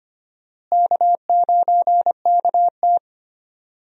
Morse code, heard as a keyed tone, about 25 wpm, 700 Hz. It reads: K9KT
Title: USA Call Signs - 4 Characters - Repeated 2x - 25wpm